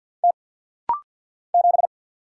Morse code, heard as a keyed tone, about 50 wpm, 700 Hz. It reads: T 7
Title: Single Letter-Number - Rapid-Fire 50wpm